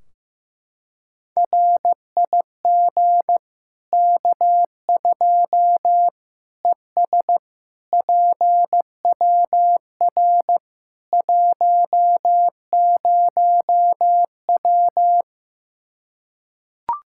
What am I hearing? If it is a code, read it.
Morse code, heard as a keyed tone, about 15 wpm, 700 Hz. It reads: RIG K2 ES PWR 10W
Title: Practice QSO - Exchange 2 - Info - 15wpm